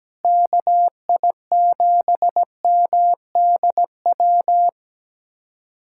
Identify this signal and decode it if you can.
Morse code, heard as a keyed tone, about 17 wpm, 700 Hz. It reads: KI7MDW
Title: USA Callsign DE Callsign - 6 Characters - 17wpm